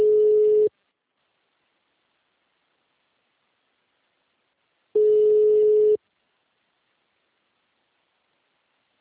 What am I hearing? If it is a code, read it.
Ringback tone (Europe, 425 Hz).
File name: out- -2012- - - .166